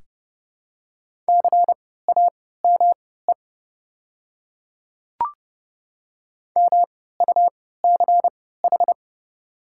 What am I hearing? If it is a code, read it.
Morse code, heard as a keyed tone, about 30 wpm, spaced as for 17 wpm, 690 Hz. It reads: CAME MUCH
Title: 3x Character Spacing - Top 200 Words - 30wpm